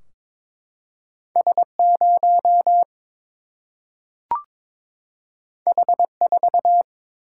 Morse code, heard as a keyed tone, about 22 wpm, 700 Hz. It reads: S0 H4
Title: Single-Letter Single-Number 22wpm